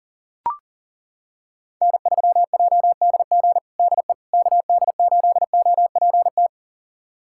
Morse code, heard as a keyed tone, about 40 wpm, 700 Hz. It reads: N3JDG DE KD8OPT
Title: USA Callsign DE Callsign 40wpm